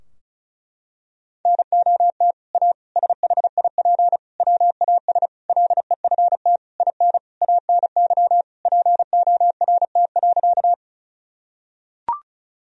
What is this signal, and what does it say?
Morse code, heard as a keyed tone, about 34 wpm, 700 Hz. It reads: NOT A SHIP WAS LEFT IN ANY PORT.
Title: Sentences from Top 1000 Words - Review Part 2 - 35wpm